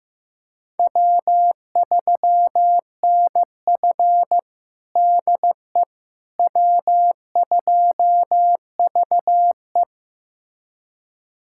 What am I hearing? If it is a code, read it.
Morse code, heard as a keyed tone, about 15 wpm, 700 Hz. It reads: W3NF DE W2VE
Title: USA Callsign DE Callsign - 4 Characters - 15wpm